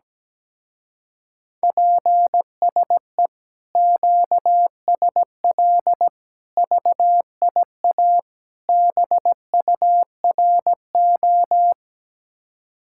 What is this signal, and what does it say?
Morse code, heard as a keyed tone, about 17 wpm, 700 Hz. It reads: PSE QSL VIA BURO
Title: Speed-Racing Common QSO Phrases Encore - 17wpm